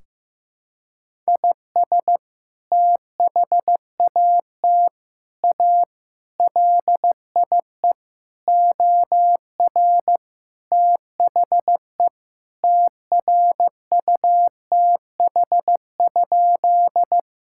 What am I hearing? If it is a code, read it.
Morse code, heard as a keyed tone, about 16 wpm, 700 Hz. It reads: IS THAT A LIE OR THE TRUTH?
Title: Sentences from Top 1600 Words 15wpm